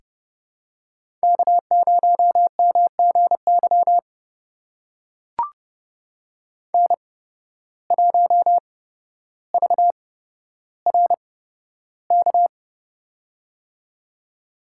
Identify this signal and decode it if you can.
Morse code, heard as a keyed tone, about 29 wpm, spaced as for 13 wpm, 700 Hz. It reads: K0MGY N1VRK
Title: USA Call Signs - 5 Characters - 8x Character Spacing - 30wpm